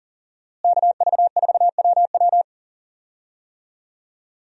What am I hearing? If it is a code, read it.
Morse code, heard as a keyed tone, about 40 wpm, 700 Hz. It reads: KV4WW
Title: USA Call Signs - 5 Characters - Repeated 2x - 40wpm